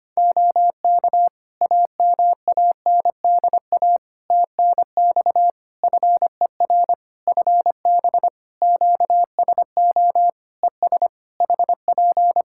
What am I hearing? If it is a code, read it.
Morse code, heard as a keyed tone, about 25 wpm, 700 Hz. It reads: OK AMANDA TNX FER FB QSO ES HP
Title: K1USN SST QSO Practice 25wpm